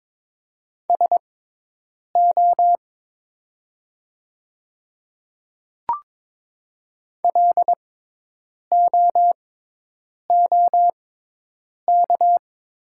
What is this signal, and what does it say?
Morse code, heard as a keyed tone, about 22 wpm, spaced as for 8 wpm, 700 Hz. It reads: SO LOOK
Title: 6x Character Spacing - Top 100 Words - 22wpm